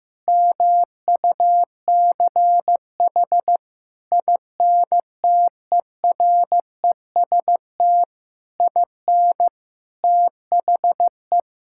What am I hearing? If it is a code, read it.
Morse code, heard as a keyed tone, about 16 wpm, 700 Hz. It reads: MUCH INTEREST IN THE
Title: Sentences from Top 1000 Words - Review Part 5 - 15wpm